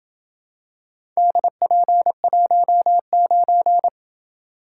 Morse code, heard as a keyed tone, about 27 wpm, 700 Hz. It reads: DP19